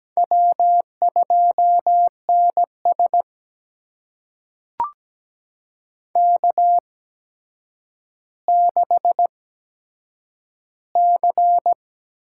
Morse code, heard as a keyed tone, about 17 wpm, spaced as for 8 wpm, 700 Hz. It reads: W2NS K6C
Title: USA Call Signs - 4 Characters - 8x Character Spacing - 17wpm